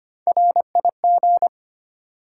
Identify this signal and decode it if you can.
Morse code, heard as a keyed tone, about 25 wpm, 700 Hz. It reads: RIG
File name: practice / 25wpm / Top 20 QSO Elements - 25wpm